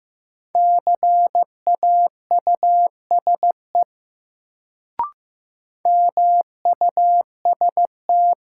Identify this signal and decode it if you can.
Morse code, heard as a keyed tone, about 15 wpm, 700 Hz. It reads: CAUSE MUST
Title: Sentences from Top 200 Words - New Word Drill - 15wpm